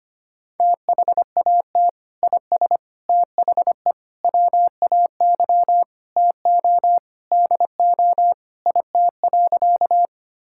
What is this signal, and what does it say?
Morse code, heard as a keyed tone, about 25 wpm, 700 Hz. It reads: THAT IS THE WAY TO DO IT.
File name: practice / 25wpm / Sentences with 3-Word Phrases - 25wpm